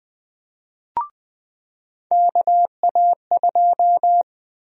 Morse code, heard as a keyed tone, about 20 wpm, 700 Hz. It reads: KA2
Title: USA Call Sign Prefixes - 20wpm